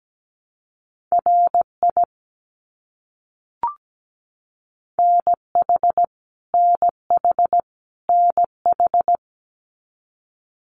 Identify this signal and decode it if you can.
Morse code, heard as a keyed tone, about 17 wpm, 700 Hz. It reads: RI NH NH NH
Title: US States Repeated 3x 17wpm